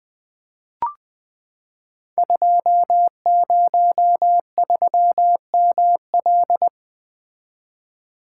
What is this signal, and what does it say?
Morse code, heard as a keyed tone, about 20 wpm, 700 Hz. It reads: 203ML
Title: Code Groups - 5 Letter:Numbers - 20wpm